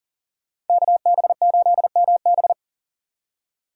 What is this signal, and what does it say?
Morse code, heard as a keyed tone, about 52 wpm, 700 Hz. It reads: KB8MB